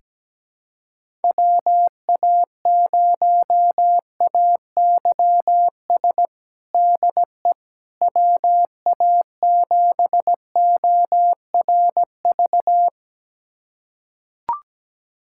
Morse code, heard as a keyed tone, about 17 wpm, 700 Hz. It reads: WA0AYS DE WA7ORV
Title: USA Callsign DE Callsign - 6 Characters - 17wpm